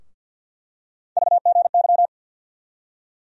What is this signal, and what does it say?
Morse code, heard as a keyed tone, about 51 wpm, 700 Hz. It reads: VGY
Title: USA Call Sign Postfixes - 3 Letters 50wpm